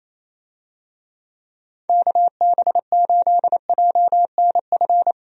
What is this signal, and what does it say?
Morse code, heard as a keyed tone, about 28 wpm, 700 Hz. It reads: KB8JNF
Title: USA Call Signs - 6 Characters - 28wpm